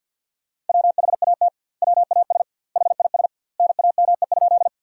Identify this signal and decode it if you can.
Morse code, heard as a keyed tone, about 52 wpm, 700 Hz. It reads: WHAT WAS HIS NAME?
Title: 2x Word-Spacing - Sentences from Top 200 Words - 50wpm